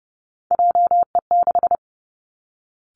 Morse code, heard as a keyed tone, about 30 wpm, 700 Hz. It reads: JE6
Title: Three Challenging Letter-Number Combinations 30wpm